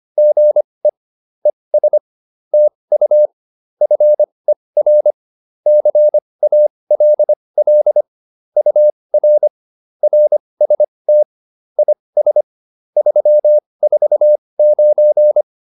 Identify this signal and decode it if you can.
Morse code, heard as a keyed tone, about 25 wpm, spaced as for 20 wpm, 600 Hz. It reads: GE ES TU FER CALL UR RST IS 349
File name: QSO106 20